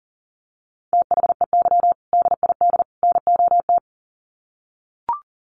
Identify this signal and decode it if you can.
Morse code, heard as a keyed tone, about 40 wpm, 700 Hz. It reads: THEY DID NOT
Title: Sets of 3 Words 40wpm